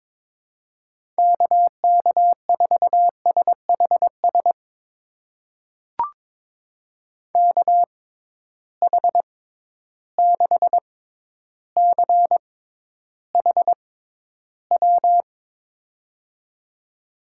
Morse code, heard as a keyed tone, about 22 wpm, spaced as for 11 wpm, 700 Hz. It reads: KK4SHS KH6CHW
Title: USA Call Signs - 6 Characters - 6x Character Spacing - 22wpm